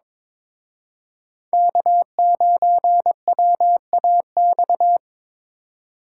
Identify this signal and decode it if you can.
Morse code, heard as a keyed tone, about 26 wpm, 700 Hz. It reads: K9WAX
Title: Speed Racing - USA 1x3 Call Signs 22wpm